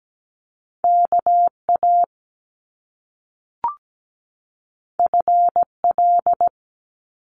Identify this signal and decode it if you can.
Morse code, heard as a keyed tone, about 17 wpm, 700 Hz. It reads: KA FL
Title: Top Two Letter Combinations in Words 17wpm